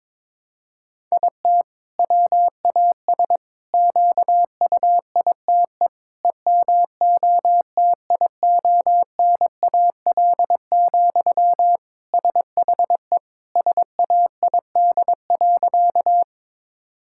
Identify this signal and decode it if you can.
Morse code, heard as a keyed tone, about 22 wpm, 700 Hz. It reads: IT WAS QUITE EMOTIONAL, SHE SAID.
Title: Sentences from Top 1000 Words - Review Part 5 - 22wpm